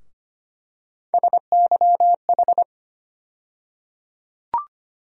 Morse code, heard as a keyed tone, about 25 wpm, 700 Hz. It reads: SYH
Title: USA Call Sign Postfixes - 3 Letters 25wpm